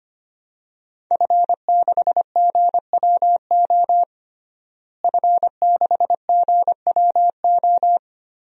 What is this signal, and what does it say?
Morse code, heard as a keyed tone, about 25 wpm, 700 Hz. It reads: F6GWO F6GWO
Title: DXCC-100 Call Signs - 5 Characters - Repeated 3x - 25wpm